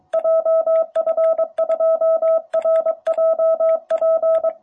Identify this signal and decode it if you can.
Morse code, heard as a keyed tone, about 23 wpm, 660 Hz. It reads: JF2RJP
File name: jh8cbh / 2021cw